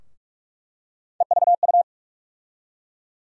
Morse code, heard as a keyed tone, about 46 wpm, 700 Hz. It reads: EVU
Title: USA Call Sign Postfixes - 3 Letters 45wpm